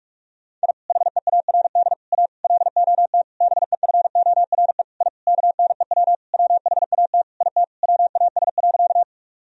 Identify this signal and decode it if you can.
Morse code, heard as a keyed tone, about 46 wpm, 700 Hz. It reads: I HEARD A LOT BEFORE I KNEW WHAT IT WAS.
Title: Sentences from Top 1000 Words - Review Part 3 - 45wpm